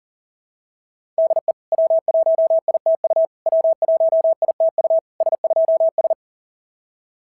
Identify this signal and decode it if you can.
Morse code, heard as a keyed tone, about 40 wpm, 690 Hz. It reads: DE W1ITU W1ITU S2S
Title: SOTA QSO Practice 40wpm